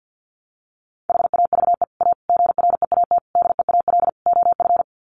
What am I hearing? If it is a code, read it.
Morse code, heard as a keyed tone, about 50 wpm, 700 Hz. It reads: HAVE A GREAT DEAL OF
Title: Sets of 5 Words Encore2 50wpm